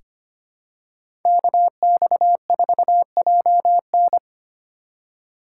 Morse code, heard as a keyed tone, about 25 wpm, 700 Hz. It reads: KX4JN
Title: USA Call Signs - 5 Characters - 4x Character Spacing - 25wpm